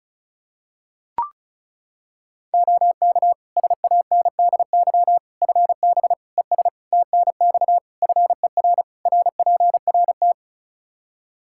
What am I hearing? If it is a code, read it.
Morse code, heard as a keyed tone, about 36 wpm, 700 Hz. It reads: OK SANDY FB ES TNX FER RPRT
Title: K1USN SST QSO Practice 35wpm